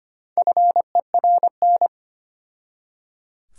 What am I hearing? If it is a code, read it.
Morse code, heard as a keyed tone, about 25 wpm, 700 Hz. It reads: FERN
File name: practice / 25wpm / Difficult Words with Hints - 25wpm